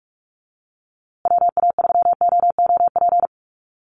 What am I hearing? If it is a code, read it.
Morse code, heard as a keyed tone, about 44 wpm, 700 Hz. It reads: WA3OOP
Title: USA Call Signs Repeated 3x 45wpm